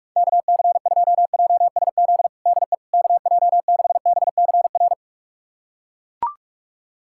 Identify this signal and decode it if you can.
Morse code, heard as a keyed tone, about 41 wpm, 700 Hz. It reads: KC2JSZ DE KJ6BCR
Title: USA Callsign DE Callsign - 6 Characters - 45wpm